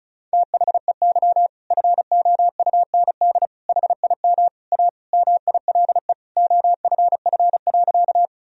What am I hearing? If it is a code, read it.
Morse code, heard as a keyed tone, about 34 wpm, 700 Hz. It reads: THEY FOUND HIM A MILE OFF.
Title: Sentences from Top 1000 Words - Review Part 1 - 35wpm